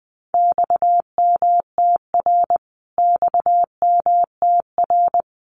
Morse code, heard as a keyed tone, about 19 wpm, 700 Hz. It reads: XMTR XMTR
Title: Abbreviations Repeated 3x - 20wpm